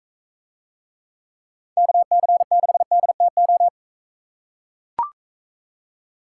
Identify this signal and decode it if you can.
Morse code, heard as a keyed tone, about 44 wpm, 700 Hz. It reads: KC6DTO